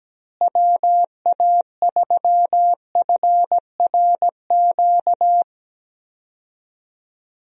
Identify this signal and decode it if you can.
Morse code, heard as a keyed tone, about 17 wpm, 700 Hz. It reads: WA3FRQ
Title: USA Call Signs - 6 Characters - Repeated 2x - 17wpm